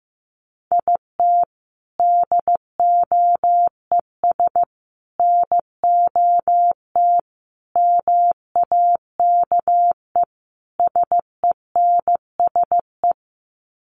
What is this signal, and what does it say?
Morse code, heard as a keyed tone, about 15 wpm, 700 Hz. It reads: IT DOES NOT MAKE SENSE
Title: Sets of 5 Words Encore2 15wpm